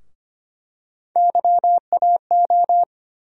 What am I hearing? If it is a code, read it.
Morse code, heard as a keyed tone, about 25 wpm, 700 Hz. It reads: YAO
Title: USA Call Sign Postfixes - 3 Letters 25wpm